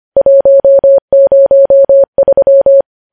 Morse code, heard as a keyed tone, about 25 wpm, 550 Hz. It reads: 103
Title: News Headlines at 25